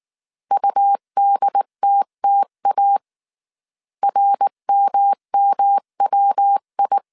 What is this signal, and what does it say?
Morse code, heard as a keyed tone, about 19 wpm, 800 Hz. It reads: UDTTA RMMWI
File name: CW / DET / 1 3 1500C S B 18wpm 1916